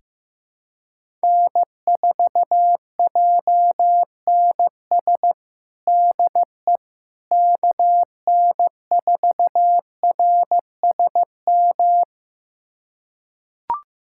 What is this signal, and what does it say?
Morse code, heard as a keyed tone, about 15 wpm, 700 Hz. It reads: N4JNS DE KN4RSM
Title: USA Callsign DE Callsign 15wpm